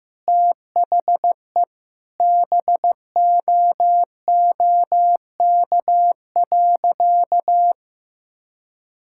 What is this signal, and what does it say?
Morse code, heard as a keyed tone, about 15 wpm, 700 Hz. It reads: THE BOOK.
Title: Sentences from Top 1000 Words - Review Part 6 - 15wpm